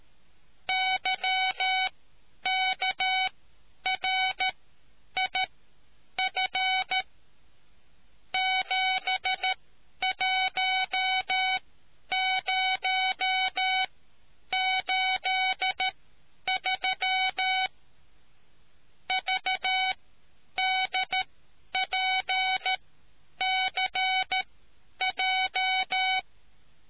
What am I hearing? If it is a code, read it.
Morse code, heard as a keyed tone, about 13 wpm, spaced as for 9 wpm, 750 Hz. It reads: YKRIF 71083 VDPCJ